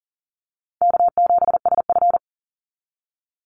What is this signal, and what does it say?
Morse code, heard as a keyed tone, about 40 wpm, 700 Hz. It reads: K7SF
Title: Easy Call Signs Repeated 3x - 40wpm